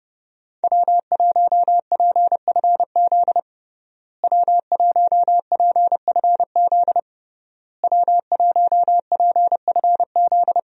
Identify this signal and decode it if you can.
Morse code, heard as a keyed tone, about 32 wpm, 700 Hz. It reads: W1PFZ W1PFZ W1PFZ
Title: USA Call Signs - 5 Characters - Repeated 3x - 30wpm